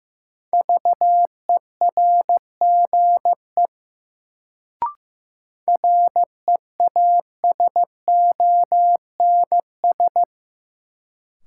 Morse code, heard as a keyed tone, about 15 wpm, 700 Hz. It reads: VERGE REASONS
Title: Sets of 4 Words - New Word Drill - 15wpm